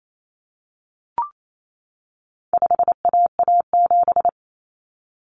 Morse code, heard as a keyed tone, about 28 wpm, 700 Hz. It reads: 5AA7
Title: Common Cut Numbers - 4-Digit 28wpm